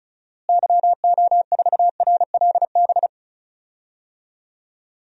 Morse code, heard as a keyed tone, about 36 wpm, 700 Hz. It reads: YO4RLB
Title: DXCC-100 Call Signs - 6 Characters - Repeated 2x - 35wpm